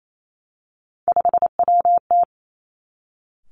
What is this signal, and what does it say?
Morse code, heard as a keyed tone, about 28 wpm, 700 Hz. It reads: 5WT
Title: Three Challenging Letter-Number Combinations 28wpm